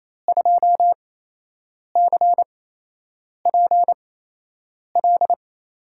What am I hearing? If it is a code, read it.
Morse code, heard as a keyed tone, about 28 wpm, spaced as for 8 wpm, 700 Hz. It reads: 2CPL
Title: USA Call Signs - 8x Character Spacing - 28wpm